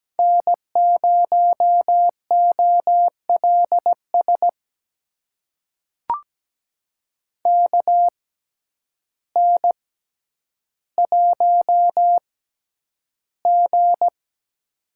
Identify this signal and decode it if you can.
Morse code, heard as a keyed tone, about 17 wpm, spaced as for 9 wpm, 700 Hz. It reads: N0OLS KN1G
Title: USA Call Signs - 6x Character Spacing - 17wpm